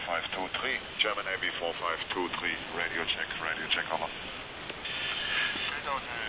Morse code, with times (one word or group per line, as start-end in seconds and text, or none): none